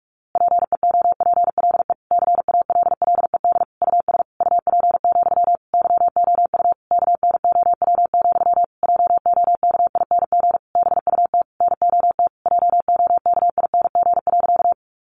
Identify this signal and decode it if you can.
Morse code, heard as a keyed tone, about 44 wpm, 700 Hz. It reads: PEOPLE CALLED US UP, YOU KNOW, JOKING BUT NOT JOKING.
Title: A Sound Of Thunder 45wpm